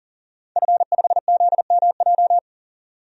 Morse code, heard as a keyed tone, about 40 wpm, 700 Hz. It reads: F5ZMJ
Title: DXCC-100 Call Signs - 5 Characters - 40wpm